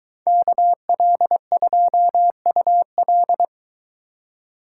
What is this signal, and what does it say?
Morse code, heard as a keyed tone, about 22 wpm, 700 Hz. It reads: KL2UL